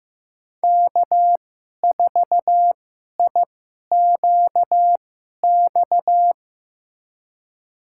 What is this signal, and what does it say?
Morse code, heard as a keyed tone, about 15 wpm, spaced as for 11 wpm, 700 Hz. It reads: K4IQX
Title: USA Call Signs - 5 Characters - 2x Character Spacing - 15wpm